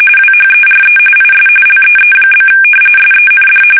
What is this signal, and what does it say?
RTTY, about 45.45 baud. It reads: WHERE IS LEE? SEND
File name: fsk850 / battleshipnj